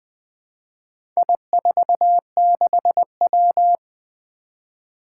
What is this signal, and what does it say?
Morse code, heard as a keyed tone, about 20 wpm, 700 Hz. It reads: I46W
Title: Four Challenging Letter-Number Combinations 20wpm